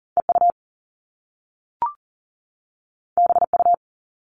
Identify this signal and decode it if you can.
Morse code, heard as a keyed tone, about 40 wpm, 700 Hz. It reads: EU BU
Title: All Cut Numbers - 2-Digit 40wpm